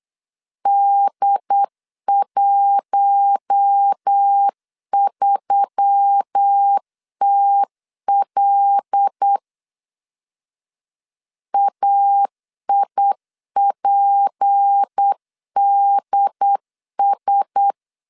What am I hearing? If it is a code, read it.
Morse code, heard as a keyed tone, about 8 wpm, 800 Hz. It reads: D13TL AIPDS
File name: CW / DET / 1 6 1500C S B 8wpm 5142